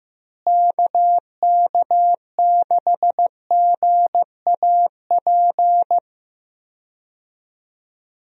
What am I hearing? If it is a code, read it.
Morse code, heard as a keyed tone, about 15 wpm, 700 Hz. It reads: KK6GAP
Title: USA Call Signs - 6 Characters - Repeated 3x - 15wpm